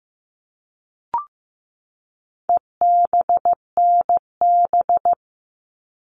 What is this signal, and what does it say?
Morse code, heard as a keyed tone, about 15 wpm, 700 Hz. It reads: EBNB
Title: All Cut Numbers - 4-Digit 15wpm